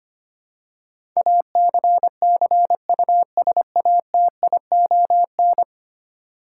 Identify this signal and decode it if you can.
Morse code, heard as a keyed tone, about 25 wpm, 700 Hz. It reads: ACCUSATION
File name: 10 Letter Words - First 3 Letters Given - 25wpm